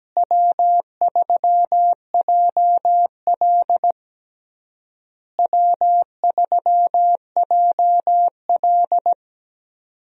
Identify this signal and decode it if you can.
Morse code, heard as a keyed tone, about 17 wpm, 700 Hz. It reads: W3JL W3JL
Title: USA Call Signs - 4 Characters - Repeated 3x - 17wpm